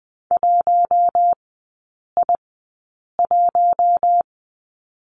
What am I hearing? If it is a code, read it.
Morse code, heard as a keyed tone, about 20 wpm, spaced as for 8 wpm, 700 Hz. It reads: 1I1
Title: 041 Lesson 8e - Number 1 - Three Characters 20wpm